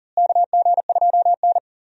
Morse code, heard as a keyed tone, about 40 wpm, 700 Hz. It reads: KG2N
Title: USA Call Signs - 4 Characters - Repeated 3x - 40wpm